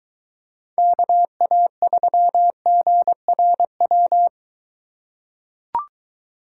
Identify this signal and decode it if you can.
Morse code, heard as a keyed tone, about 23 wpm, 700 Hz. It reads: KA3GRW